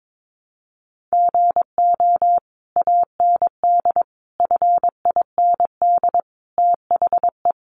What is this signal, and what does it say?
Morse code, heard as a keyed tone, about 22 wpm, 700 Hz. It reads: GO AND FIND THE
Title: Sentences from Top 1200 Words 22wpm